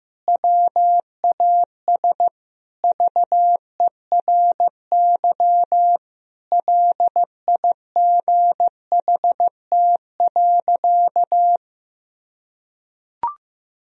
Morse code, heard as a keyed tone, about 16 wpm, 700 Hz. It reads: WAS VERY LIGHT.
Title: Sentences from Top 1000 Words - Review Part 4 - 15wpm